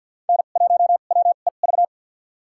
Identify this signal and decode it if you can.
Morse code, heard as a keyed tone, about 52 wpm, spaced as for 37 wpm, 700 Hz. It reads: N1WEV
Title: USA Call Signs - 5 Characters - 2x Character Spacing - 50wpm